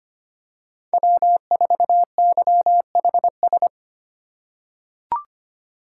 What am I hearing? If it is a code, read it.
Morse code, heard as a keyed tone, about 25 wpm, 700 Hz. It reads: W4YHS
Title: USA Call Signs - 2x Character Spacing - 25wpm